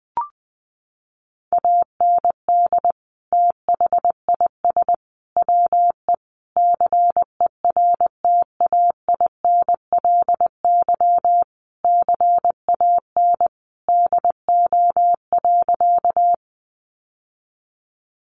Morse code, heard as a keyed tone, about 19 wpm, 700 Hz. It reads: AND THIS WE CERTAINLY CAN DO.
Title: Sentences from Top 900 Words 20wpm